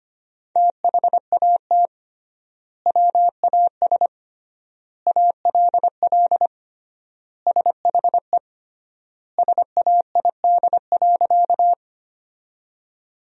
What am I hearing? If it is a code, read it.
Morse code, heard as a keyed tone, about 25 wpm, 700 Hz. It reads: THAT WAS ALL SHE SAID.
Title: 3x Word-Spacing - Sentences from Top 100 Words - 25wpm